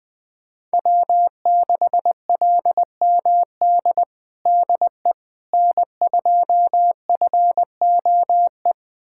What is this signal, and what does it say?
Morse code, heard as a keyed tone, about 20 wpm, 700 Hz. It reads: W6LMD DE N2FOE
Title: USA Callsign DE Callsign - 5 Characters - 20wpm